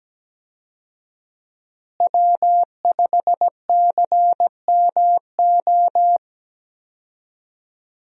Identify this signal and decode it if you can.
Morse code, heard as a keyed tone, about 17 wpm, 700 Hz. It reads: W5CMO
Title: USA Call Signs - 5 Characters - Repeated 2x - 17wpm